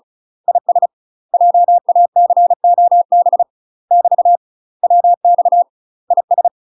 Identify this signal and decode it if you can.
Morse code, heard as a keyed tone, about 34 wpm, 700 Hz. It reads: IS JACOB <BT> WX IS
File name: UK MASTER QSOs TRAINING / 35WPM / UK MASTER QSOs 0026